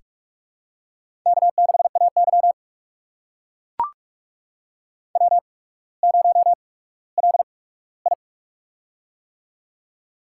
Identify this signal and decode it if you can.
Morse code, heard as a keyed tone, about 44 wpm, spaced as for 19 wpm, 700 Hz. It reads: K6AY W0LI
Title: USA Call Signs - 4 Characters - 8x Character Spacing - 45wpm